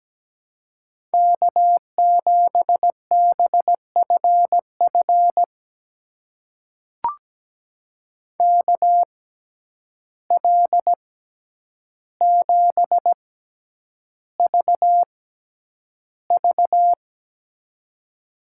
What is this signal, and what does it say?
Morse code, heard as a keyed tone, about 17 wpm, spaced as for 9 wpm, 700 Hz. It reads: K7BFF KL7VV
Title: USA Call Signs - 5 Characters - 6x Character Spacing - 17wpm